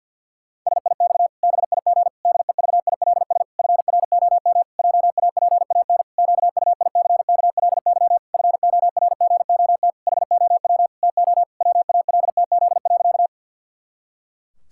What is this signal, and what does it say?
Morse code, heard as a keyed tone, about 51 wpm, 700 Hz. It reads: SIX BIG DEVILS FROM JAPAN QUICKLY FORGOT HOW TO WALTZ.
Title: Pangram Challenge v2 - 50wpm